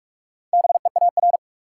Morse code, heard as a keyed tone, about 42 wpm, 700 Hz. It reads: BEAR
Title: Four Letter Words - 45wpm